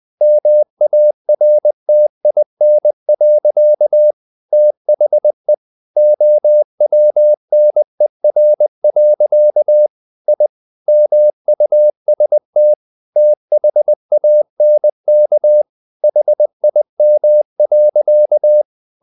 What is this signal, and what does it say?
Morse code, heard as a keyed tone, about 20 wpm, 600 Hz. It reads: MARTIN. THE OWNER. I MUST THANK HIM.
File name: the old man and the sea / the old man and the sea part 1 20wpm